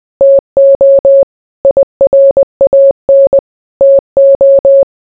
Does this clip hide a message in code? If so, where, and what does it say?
Yes: Morse code from 0.2 to 4.8 s, heard as a keyed tone, about 20 wpm, 550 Hz. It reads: TO IRAN TO